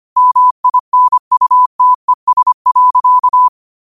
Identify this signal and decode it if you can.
Morse code, heard as a keyed tone, about 25 wpm, 1000 Hz. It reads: MINUTES.
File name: qotdfiles / 25wpm